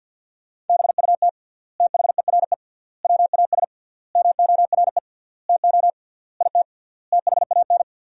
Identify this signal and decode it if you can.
Morse code, heard as a keyed tone, about 50 wpm, 700 Hz. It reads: BUT THERE WAS MORE TO IT THAN
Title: 3x Word-Spacing - Sentences from Top 100 Words - 50wpm